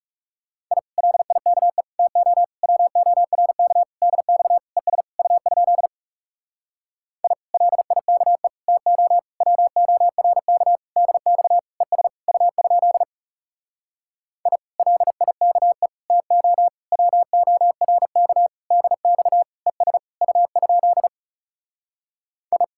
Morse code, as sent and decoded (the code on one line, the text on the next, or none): .. / .-.. .. -.- . / - --- / .-- --- .-. -.- / -.. -..- / . ... / ..- ..--.. / .. / .-.. .. -.- . / - --- / .-- --- .-. -.- / -.. -..- / . ... / ..- ..--.. / .. / .-.. .. -.- . / - --- / .-- --- .-. -.- / -.. -..- / . ... / ..- ..--.. / ..
I LIKE TO WORK DX ES U? I LIKE TO WORK DX ES U? I LIKE TO WORK DX ES U? I